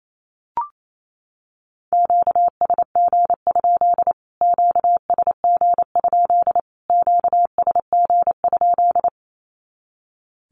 Morse code, heard as a keyed tone, about 28 wpm, 700 Hz. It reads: QSG? QSG? QSG?
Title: Q-Codes Repeated 3x 28wpm